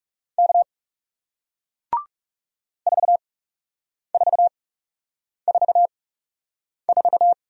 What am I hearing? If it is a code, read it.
Morse code, heard as a keyed tone, about 40 wpm, spaced as for 8 wpm, 700 Hz. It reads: K 4444